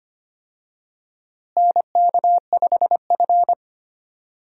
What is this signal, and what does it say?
Morse code, heard as a keyed tone, about 25 wpm, 700 Hz. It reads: NK5F
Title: USA Call Signs - 4 Characters - Repeated 3x - 25wpm